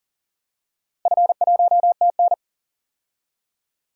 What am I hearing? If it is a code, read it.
Morse code, heard as a keyed tone, about 40 wpm, 700 Hz. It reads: F1TN
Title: DXCC-100 Call Signs - 4 Characters - Repeated 2x - 40wpm